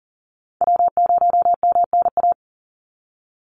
Morse code, heard as a keyed tone, about 40 wpm, 700 Hz. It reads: W0MNA
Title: NA SOTA Chaser Callsigns 40wpm